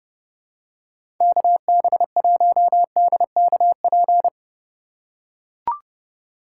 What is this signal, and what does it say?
Morse code, heard as a keyed tone, about 30 wpm, 700 Hz. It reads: KB1DKP
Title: USA Call Signs - 6 Characters - 4x Character Spacing - 30wpm